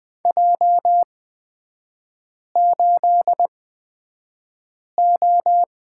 Morse code, heard as a keyed tone, about 20 wpm, spaced as for 5 wpm, 700 Hz. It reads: J8O
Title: 250 Lesson 40e - BK - Three Characters 20wpm